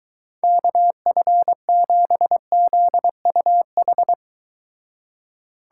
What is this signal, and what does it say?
Morse code, heard as a keyed tone, about 22 wpm, 700 Hz. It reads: KF7ZUH